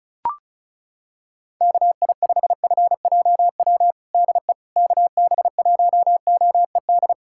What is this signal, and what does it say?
Morse code, heard as a keyed tone, about 34 wpm, 700 Hz. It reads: KI5FJW DE KB1OED
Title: USA Callsign DE Callsign - 6 Characters - 35wpm